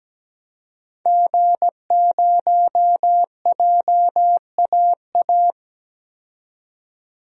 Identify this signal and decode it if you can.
Morse code, heard as a keyed tone, about 17 wpm, 700 Hz. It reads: G0JAA
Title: DXCC-100 Call Signs - 5 Characters - 17wpm